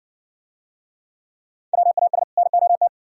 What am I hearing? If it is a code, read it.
Morse code, heard as a keyed tone, about 60 wpm, 700 Hz. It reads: WAS NOT